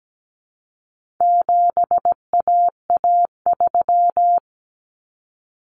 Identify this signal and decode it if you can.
Morse code, heard as a keyed tone, about 17 wpm, 700 Hz. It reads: 7AA3
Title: Common Cut Numbers - 4-Digit 17wpm